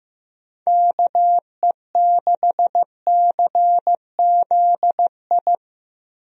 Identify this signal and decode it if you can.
Morse code, heard as a keyed tone, about 15 wpm, 700 Hz. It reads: KE6CZI
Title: USA Call Signs - 6 Characters - Repeated 2x - 15wpm